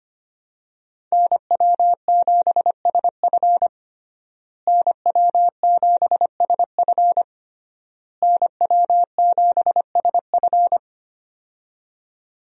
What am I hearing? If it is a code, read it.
Morse code, heard as a keyed tone, about 25 wpm, 700 Hz. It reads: NW7SF NW7SF NW7SF
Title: USA Call Signs - 5 Characters - Repeated 3x - 25wpm